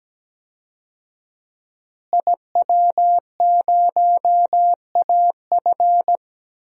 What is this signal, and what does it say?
Morse code, heard as a keyed tone, about 17 wpm, 700 Hz. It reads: IW0AF
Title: DXCC-100 Call Signs - 5 Characters - Repeated 2x - 17wpm